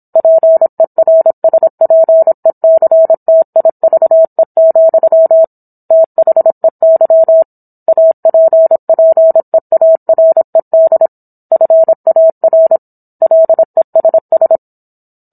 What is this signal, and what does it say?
Morse code, heard as a keyed tone, about 27 wpm, 650 Hz. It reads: PERSPECTIVE, THEY APPEARED FAR LESS